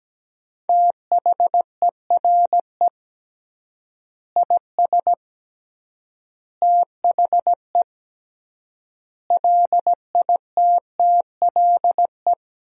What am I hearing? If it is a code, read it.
Morse code, heard as a keyed tone, about 17 wpm, 700 Hz. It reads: THERE IS THE LITTLE
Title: 3x Word-Spacing - Sentences from Top 200 Words - 17wpm